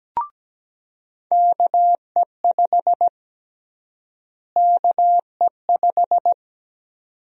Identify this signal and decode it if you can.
Morse code, heard as a keyed tone, about 17 wpm, 700 Hz. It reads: KE5 KE5
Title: USA Call Sign Prefixes - Repeated 2x - 17wpm